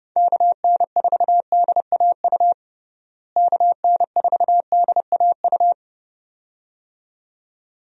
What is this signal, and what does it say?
Morse code, heard as a keyed tone, about 32 wpm, 700 Hz. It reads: KN4DAU KN4DAU
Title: USA Call Signs - 6 Characters - Repeated 2x - 30wpm